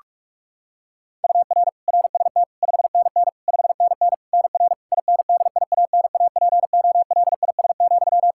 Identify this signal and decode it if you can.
Morse code, heard as a keyed tone, about 44 wpm, 700 Hz. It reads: UR RST 5NN 5NN NR INDIANAPOLIS,
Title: K1USN SST QSO Practice 45wpm